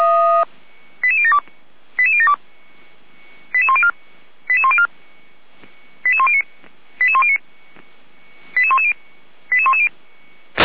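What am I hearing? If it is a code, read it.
Five-tone selective call (ZVEI), heard as quick runs of five tones, tones of about 80 ms. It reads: 88072 88072 88104 88104 88109 88109 88110 88110
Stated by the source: ZVEI five-tone call 88012 received on SDR, recorded by fire brigade Feuerwehr Orlamünde.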